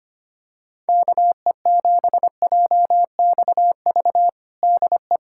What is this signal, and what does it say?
Morse code, heard as a keyed tone, about 25 wpm, 700 Hz. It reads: KE7JXV DE
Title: USA Callsign DE Callsign 25wpm